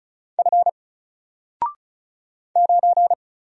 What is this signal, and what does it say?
Morse code, heard as a keyed tone, about 36 wpm, 700 Hz. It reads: F 9
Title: Single Letter-Number Alt 35wpm